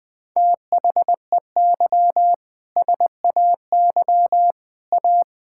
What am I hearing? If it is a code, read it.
Morse code, heard as a keyed tone, about 20 wpm, 700 Hz. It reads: THEY SAY A
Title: Sentences with 7-Word Phrases - 20wpm